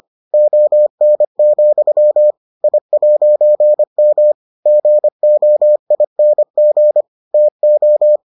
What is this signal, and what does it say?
Morse code, heard as a keyed tone, about 25 wpm, 600 Hz. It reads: ON, I'M GOING TO